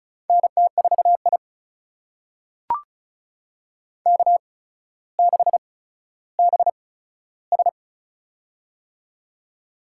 Morse code, heard as a keyed tone, about 34 wpm, spaced as for 15 wpm, 700 Hz. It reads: NT4I K6BS
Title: USA Call Signs - 4 Characters - 8x Character Spacing - 35wpm